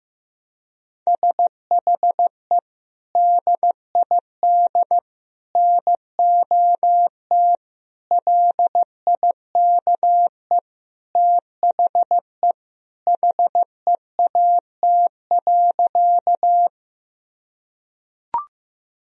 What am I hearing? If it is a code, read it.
Morse code, heard as a keyed tone, about 15 wpm, 700 Hz. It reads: SHE DID NOT LIKE THE HEAT.
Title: Sentences from Top 1000 Words - Review Part 3 - 15wpm